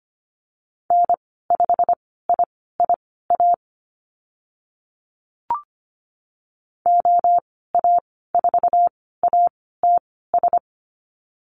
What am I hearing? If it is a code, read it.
Morse code, heard as a keyed tone, about 25 wpm, spaced as for 16 wpm, 700 Hz. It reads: N5IIA OA4ATS
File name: Lesson 15g - Number 5 - Call Signs 25wpm